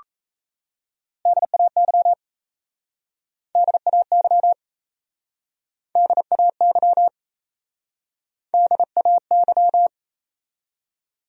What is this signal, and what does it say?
Morse code, heard as a keyed tone, about 37 wpm, 700 Hz. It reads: DAY DAY DAY DAY